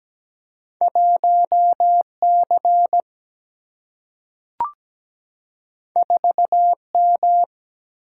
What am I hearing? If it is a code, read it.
Morse code, heard as a keyed tone, about 17 wpm, 700 Hz. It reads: 1 C 4 M
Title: Single-Letter Single-Number 17wpm